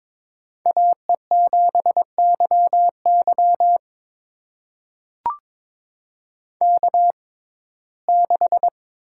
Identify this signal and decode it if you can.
Morse code, heard as a keyed tone, about 21 wpm, spaced as for 16 wpm, 700 Hz. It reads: AE7YY K6
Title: USA Call Signs - 6x Character Spacing - 22wpm